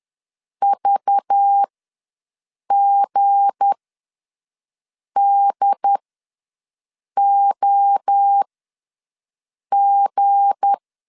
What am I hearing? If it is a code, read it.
Morse code, heard as a keyed tone, about 10 wpm, spaced as for 5 wpm, 800 Hz. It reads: VGDOG
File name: LEZIONI / 7 GF9Z / 7 G ws 10wpm 3843